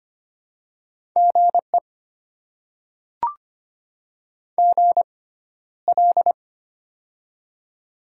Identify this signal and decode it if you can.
Morse code, heard as a keyed tone, about 25 wpm, spaced as for 13 wpm, 700 Hz. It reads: GE GL
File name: Top 90 QSO Elements - 6x Character Spacing - 25wpm